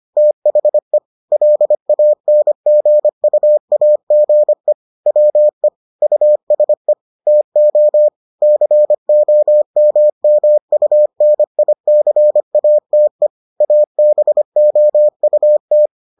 Morse code, heard as a keyed tone, about 25 wpm, 600 Hz. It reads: THE LANGUAGE WE USE TO COMMUNICATE ABOUT